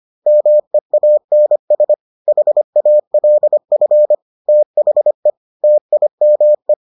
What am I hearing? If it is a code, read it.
Morse code, heard as a keyed tone, about 25 wpm, 600 Hz. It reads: MEANS HALF THE TIME